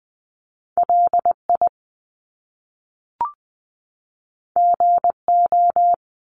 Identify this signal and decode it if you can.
Morse code, heard as a keyed tone, about 20 wpm, 700 Hz. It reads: LI GO
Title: Top Two Letter Combinations in Words 20wpm